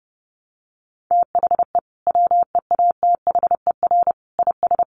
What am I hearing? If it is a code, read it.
Morse code, heard as a keyed tone, about 30 wpm, 700 Hz. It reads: THE WEATHER IS
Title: Sentences from Top 800 Words 30wpm